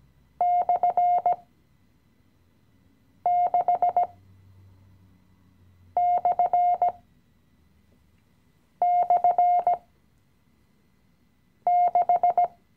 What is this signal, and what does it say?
Morse code, heard as a keyed tone, about 16 wpm, spaced as for 4 wpm, 710 Hz. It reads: /6//6
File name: CW-Online-Kurs / Anfänger / Lektion 016